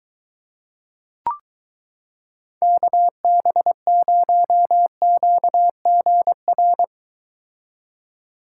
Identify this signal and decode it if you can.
Morse code, heard as a keyed tone, about 22 wpm, 700 Hz. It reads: KB0QGR